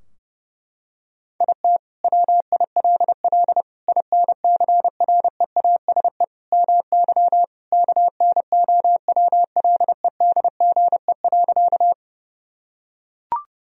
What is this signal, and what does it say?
Morse code, heard as a keyed tone, about 30 wpm, 700 Hz. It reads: IT WILL INCREASE MY KNOWLEDGE.
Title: Sentences from Top 1000 Words - Review Part 4 - 30wpm